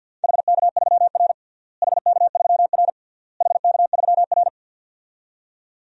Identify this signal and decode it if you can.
Morse code, heard as a keyed tone, about 52 wpm, 700 Hz. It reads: HK3R HK3R HK3R
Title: DXCC-100 Call Signs - 4 Characters - Repeated 3x - 50wpm